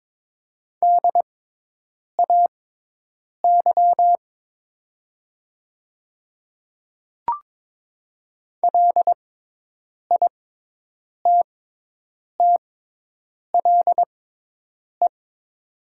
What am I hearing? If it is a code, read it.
Morse code, heard as a keyed tone, about 22 wpm, spaced as for 8 wpm, 700 Hz. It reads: DAY LITTLE
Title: 6x Character Spacing - Top 300 Words - 22wpm